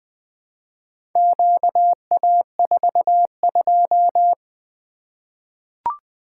Morse code, heard as a keyed tone, about 20 wpm, 700 Hz. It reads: QA42
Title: Four Letter-Number Combinations - 20wpm